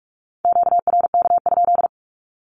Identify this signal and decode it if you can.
Morse code, heard as a keyed tone, about 46 wpm, 700 Hz. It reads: QRK?
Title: Q-Codes with Spoken Meaning - 45wpm